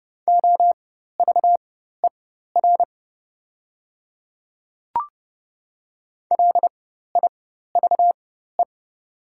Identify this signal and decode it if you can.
Morse code, heard as a keyed tone, about 30 wpm, spaced as for 14 wpm, 700 Hz. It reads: OVER LIVE
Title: 4x Character Spacing - Top 200 Words - 30wpm